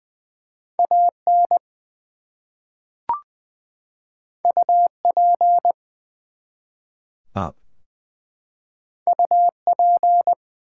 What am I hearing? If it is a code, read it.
Morse code, heard as a keyed tone, about 20 wpm, 700 Hz. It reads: AN UP UP
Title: Two Letter Words - 20wpm